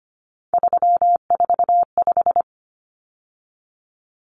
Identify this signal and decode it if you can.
Morse code, heard as a keyed tone, about 25 wpm, 700 Hz. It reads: 345
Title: Common Cut Numbers - 3-Digit 25wpm